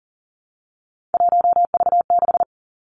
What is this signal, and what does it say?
Morse code, heard as a keyed tone, about 40 wpm, 700 Hz. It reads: 1V6
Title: Three Challenging Letter-Number Combinations 40wpm